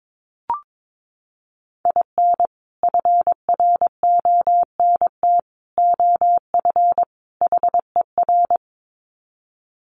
Morse code, heard as a keyed tone, about 22 wpm, 700 Hz. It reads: IN FRONT OF HER
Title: Sets of 4 Words 22wpm